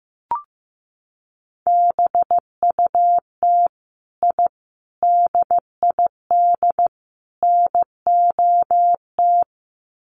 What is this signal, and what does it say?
Morse code, heard as a keyed tone, about 15 wpm, 700 Hz. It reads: BUT I DID NOT
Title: Sets of 4 Words 15wpm